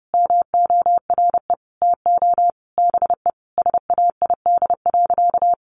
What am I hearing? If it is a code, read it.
Morse code, heard as a keyed tone, about 30 wpm, 700 Hz. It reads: MORE TO BE SAID.
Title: Sentences from Top 1000 Words 30wpm